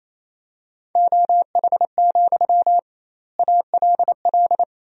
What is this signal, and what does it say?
Morse code, heard as a keyed tone, about 28 wpm, 700 Hz. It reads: OH, ALL
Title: 2x Word-Spacing - Sentences from Top 500 Words - 28wpm